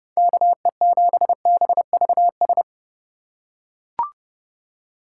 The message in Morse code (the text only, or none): KE7BVS